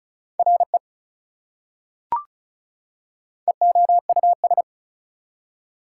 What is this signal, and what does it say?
Morse code, heard as a keyed tone, about 36 wpm, 700 Hz. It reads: RE EOUS
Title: Prefixes and Suffixes 35wpm